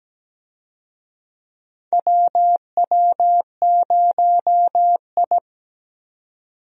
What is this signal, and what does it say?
Morse code, heard as a keyed tone, about 17 wpm, 700 Hz. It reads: WW0I
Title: USA Call Signs - 4 Characters - 17wpm